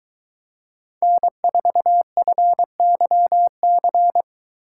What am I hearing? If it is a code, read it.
Morse code, heard as a keyed tone, about 23 wpm, 700 Hz. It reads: N4FYC